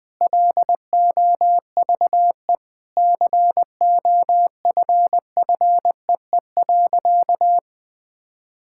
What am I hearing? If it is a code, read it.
Morse code, heard as a keyed tone, about 19 wpm, 700 Hz. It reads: LOVE COFFEE.
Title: Sentences with 7-Word Phrases - Spoken Beginning - 20wpm